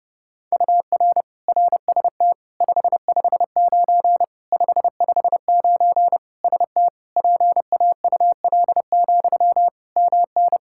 Morse code, heard as a keyed tone, about 32 wpm, 700 Hz. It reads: UR RST 559 559 ST PAUL, MN
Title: Practice QSO - Exchange 1 - Beginning - 30wpm